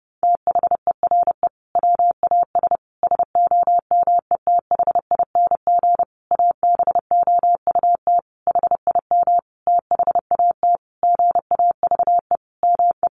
Morse code, heard as a keyed tone, about 30 wpm, 700 Hz. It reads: THERE WAS SOMETHING ABOUT HIM THAT GAVE ME
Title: Sentences from Top 1500 Words 30wpm